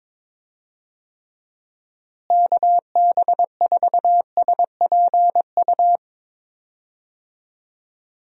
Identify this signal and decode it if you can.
Morse code, heard as a keyed tone, about 22 wpm, 700 Hz. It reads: KB4SPU
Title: USA Call Signs - 6 Characters - Repeated 3x - 22wpm